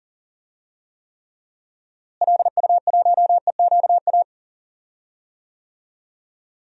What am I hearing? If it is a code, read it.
Morse code, heard as a keyed tone, about 40 wpm, 700 Hz. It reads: LU1EQA
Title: DXCC-100 Call Signs - 6 Characters - Repeated 3x - 40wpm